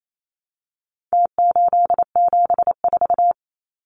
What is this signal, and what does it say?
Morse code, heard as a keyed tone, about 28 wpm, 700 Hz. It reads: T874
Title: Numbers - Contest Style - 4-Digit 28wpm